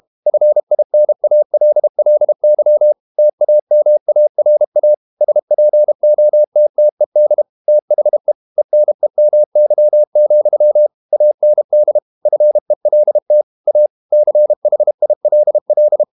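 Morse code, heard as a keyed tone, about 32 wpm, 600 Hz. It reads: FINALLY TAMARA SPOTTED THE ENEMY, AND FELT A CHILL
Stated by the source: Morse code practice at 32 wpm from CW med Gnister, reading Ken Follett's novel Never in English.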